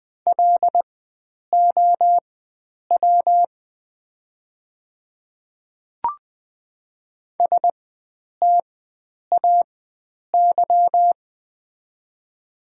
Morse code, heard as a keyed tone, about 20 wpm, spaced as for 9 wpm, 700 Hz. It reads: LOW STAY
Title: 4x Character Spacing - Top 500 Words - 20wpm